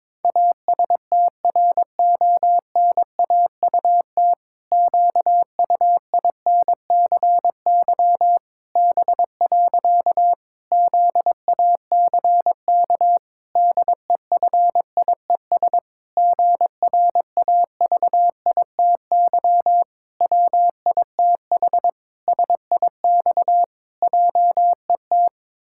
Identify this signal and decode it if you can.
Morse code, heard as a keyed tone, about 22 wpm, 700 Hz. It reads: ASTRONAUT QUINCY B. ZACK DEFIES GRAVITY WITH SIX JET
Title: Pangram Challenge v2 - 22wpm